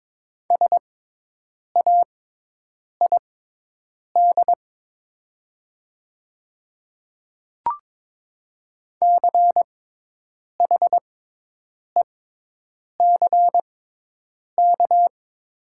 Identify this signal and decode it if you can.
Morse code, heard as a keyed tone, about 22 wpm, spaced as for 8 wpm, 700 Hz. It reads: SAID CHECK
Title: 6x Character Spacing - Top 500 Words - 22wpm